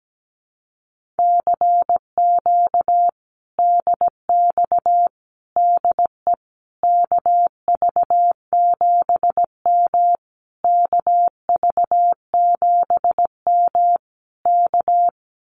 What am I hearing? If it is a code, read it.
Morse code, heard as a keyed tone, about 17 wpm, 700 Hz. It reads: CQ DX DE KV7M KV7M K
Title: Common QSO Phrases Encore 17wpm